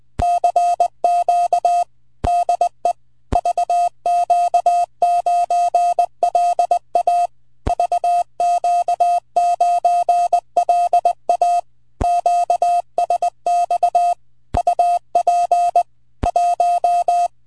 Morse code, heard as a keyed tone, about 20 wpm, 700 Hz. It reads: CQ DE VQ9LA VQ9LA QSX UP 1
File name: vq9la / N4IS 27AUG08 2348Z 160MTR CW